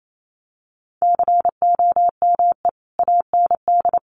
Morse code, heard as a keyed tone, about 28 wpm, 700 Hz. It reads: COME AND
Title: Sentences from Top 1400 Words 28wpm